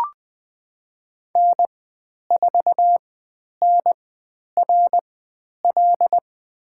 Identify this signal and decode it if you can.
Morse code, heard as a keyed tone, about 20 wpm, spaced as for 10 wpm, 700 Hz. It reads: N4NRL
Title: 109 Lesson 18g - Letter m - Call Signs 20wpm